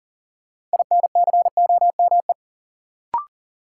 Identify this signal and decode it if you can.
Morse code, heard as a keyed tone, about 40 wpm, 700 Hz. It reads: INCOME
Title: Sentences from Top 1000 Words - Review Part 5 - New Word Drill - 40wpm